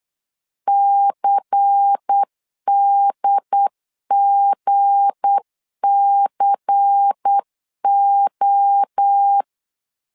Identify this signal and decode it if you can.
Morse code, heard as a keyed tone, about 8 wpm, 800 Hz. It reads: CDGCO
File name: CW / DET / 1 9 1500C S A 8wpm 5637